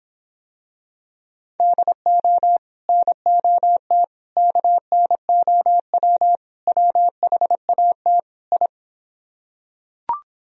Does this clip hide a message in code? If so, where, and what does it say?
Yes: Morse code from 1.6 to 8.7 s, heard as a keyed tone, about 26 wpm, 700 Hz. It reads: DO NOT KNOW WHAT I